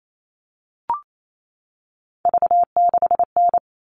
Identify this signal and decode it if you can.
Morse code, heard as a keyed tone, about 28 wpm, 700 Hz. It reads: V6N
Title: All Cut Numbers - 3-Digit 28wpm